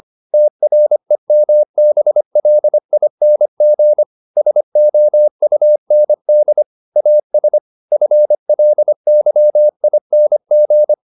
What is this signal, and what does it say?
Morse code, heard as a keyed tone, about 25 wpm, 600 Hz. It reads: TREMBLING SOUND AS FLYING
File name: the old man and the sea part 1 25wpm